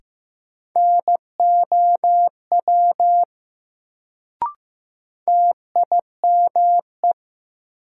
Morse code, heard as a keyed tone, about 15 wpm, 700 Hz. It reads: NOW TIME
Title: Sentences from Top 100 Words - New Word Drill - 15wpm